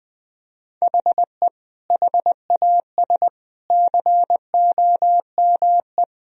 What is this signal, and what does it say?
Morse code, heard as a keyed tone, about 20 wpm, 700 Hz. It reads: HE HAS COME
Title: Sentences from Top 1000 Words - Review Part 4 - 20wpm